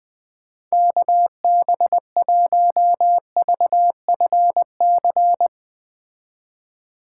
Morse code, heard as a keyed tone, about 20 wpm, 700 Hz. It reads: KB1VFC
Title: Speed Racing - USA 2x3 Call Signs 17wpm